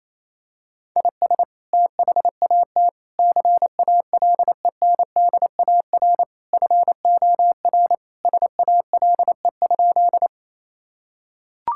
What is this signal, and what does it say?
Morse code, heard as a keyed tone, about 28 wpm, 700 Hz. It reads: IS THAT CALENDAR FOR SALE?
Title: Sentences from Top 1600 Words 28wpm